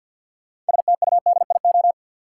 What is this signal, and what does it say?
Morse code, heard as a keyed tone, about 52 wpm, 700 Hz. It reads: STUDIO
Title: Sentences from Top 1200 Words - New Word Drill - 50wpm